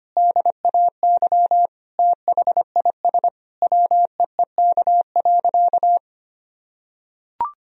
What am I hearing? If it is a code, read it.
Morse code, heard as a keyed tone, about 25 wpm, 700 Hz. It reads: DAY THIS WEEK.
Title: Sentences from Top 1000 Words - Review Part 2 - 25wpm